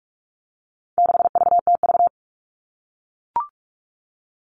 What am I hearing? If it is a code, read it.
Morse code, heard as a keyed tone, about 44 wpm, 700 Hz. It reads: 6VTV
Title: All Cut Numbers - 4-Digit 45wpm